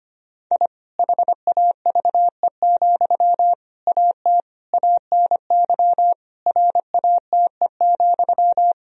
Morse code, heard as a keyed tone, about 25 wpm, 700 Hz. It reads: I HAVE, AT ANY RATE,
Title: Sentences from Top 1000 Words - Review Part 4 - 25wpm